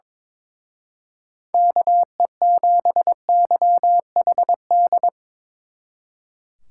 Morse code, heard as a keyed tone, about 25 wpm, 700 Hz. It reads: KE7YHD
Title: Speed Racing - USA 2x3 Call Signs 22wpm